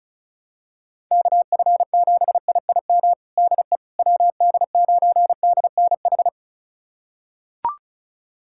Morse code, heard as a keyed tone, about 34 wpm, 700 Hz. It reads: KF7IIM DE WD9DNH
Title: USA Callsign DE Callsign - 6 Characters - 35wpm